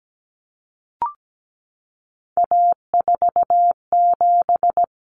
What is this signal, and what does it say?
Morse code, heard as a keyed tone, about 17 wpm, 700 Hz. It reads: A47
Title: Common Cut Numbers - 3-Digit 17wpm